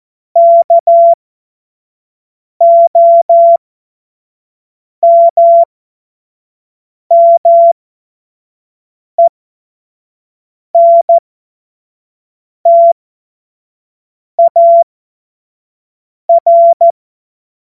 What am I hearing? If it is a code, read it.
Morse code, heard as a keyed tone, about 14 wpm, spaced as for 5 wpm, 680 Hz. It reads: KOMMENTAR
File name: Morsekurs / Witze / Witz 19. April 2022 25 F70